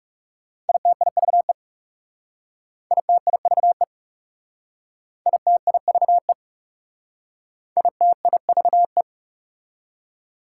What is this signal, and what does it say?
Morse code, heard as a keyed tone, about 38 wpm, 690 Hz. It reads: ITIVE ITIVE ITIVE ITIVE
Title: Speed-Racing Prefixes and Suffixes 30wpm